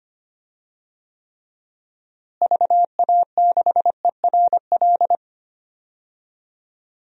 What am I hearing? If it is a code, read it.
Morse code, heard as a keyed tone, about 25 wpm, 700 Hz. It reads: VA6ERL
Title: DXCC-100 Call Signs - 6 Characters - Repeated 2x - 25wpm